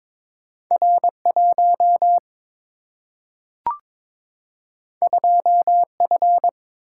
Morse code, heard as a keyed tone, about 22 wpm, 700 Hz. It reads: R 1 2 F
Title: Single-Letter Single-Number 22wpm